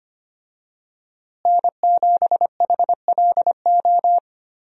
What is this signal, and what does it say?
Morse code, heard as a keyed tone, about 25 wpm, 700 Hz. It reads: N7HLO
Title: USA Call Signs - 5 Characters - Repeated 2x - 25wpm